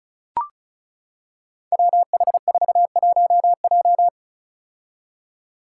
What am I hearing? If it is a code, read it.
Morse code, heard as a keyed tone, about 36 wpm, 700 Hz. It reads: WH41J
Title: Five Challenging Letter-Number Combinations 35wpm